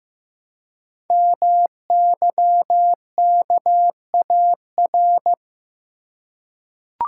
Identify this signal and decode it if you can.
Morse code, heard as a keyed tone, about 15 wpm, 700 Hz. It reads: MYKAR
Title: Code Groups - 5 Letters - 15wpm